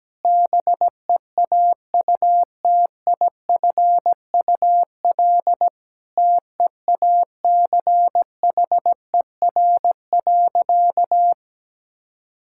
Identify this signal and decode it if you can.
Morse code, heard as a keyed tone, about 17 wpm, 700 Hz. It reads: BEAUTIFUL TEACHER.
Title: Sentences from Top 1000 Words - Review Part 6 - 17wpm